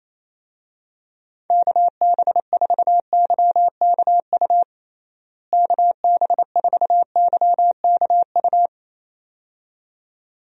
Morse code, heard as a keyed tone, about 28 wpm, 700 Hz. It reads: KB4YKU KB4YKU
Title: USA Call Signs - 6 Characters - Repeated 2x - 28wpm